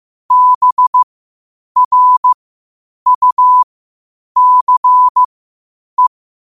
Morse code, heard as a keyed tone, about 15 wpm, spaced as for 8 wpm, 1000 Hz. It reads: BRUCE
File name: qotdfiles / 8wpm